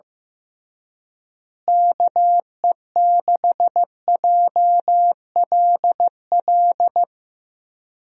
Morse code, heard as a keyed tone, about 15 wpm, 700 Hz. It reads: KE6JLL
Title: USA Call Signs - 6 Characters - Repeated 3x - 15wpm